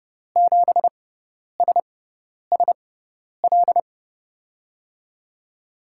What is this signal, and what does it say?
Morse code, heard as a keyed tone, about 32 wpm, spaced as for 10 wpm, 700 Hz. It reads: 7SSL
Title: USA Call Signs - 5 Characters - 6x Character Spacing - 30wpm